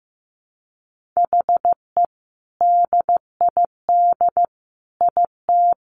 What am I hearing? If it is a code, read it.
Morse code, heard as a keyed tone, about 15 wpm, 700 Hz. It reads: HE DID IT
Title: Sentences from Top 1500 Words 15wpm